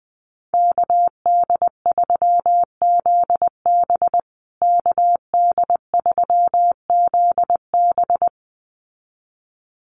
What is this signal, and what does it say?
Morse code, heard as a keyed tone, about 20 wpm, 700 Hz. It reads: KD3ZB KD3ZB
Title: USA Call Signs Repeated 2x 20wpm